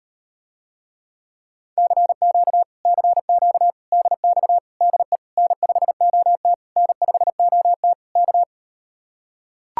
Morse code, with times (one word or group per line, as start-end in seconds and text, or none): 1.8-2.6 CQ
2.9-3.7 CQ
3.9-4.6 DX
4.8-5.2 DE
5.4-6.5 N5OT
6.8-7.9 N5OT
8.2-8.4 K